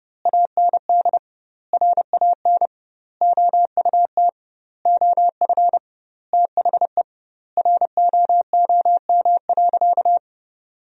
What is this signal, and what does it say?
Morse code, heard as a keyed tone, about 31 wpm, 700 Hz. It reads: AND RAN OUT OF THE ROOM.
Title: 2x Word-Spacing - Sentences from Top 500 Words - 30wpm